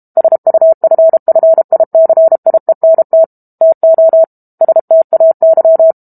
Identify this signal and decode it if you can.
Morse code, heard as a keyed tone, about 32 wpm, 650 Hz. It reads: SUFFICIENT TO STAY
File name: Book Two / 06-Chapter 6 - 32 WPM